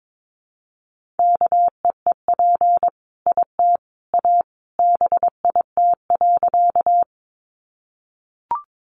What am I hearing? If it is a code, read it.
Morse code, heard as a keyed tone, about 22 wpm, 700 Hz. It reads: KEEP IT A BIT.
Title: Sentences from Top 900 Words 22wpm